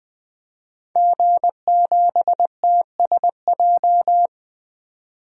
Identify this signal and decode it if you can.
Morse code, heard as a keyed tone, about 20 wpm, 700 Hz. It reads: G7TSJ
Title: DXCC-100 Call Signs - 5 Characters - 20wpm